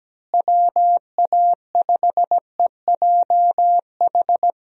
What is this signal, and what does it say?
Morse code, heard as a keyed tone, about 17 wpm, 700 Hz. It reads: WA5EJH
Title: USA Call Signs - 6 Characters - Repeated 2x - 17wpm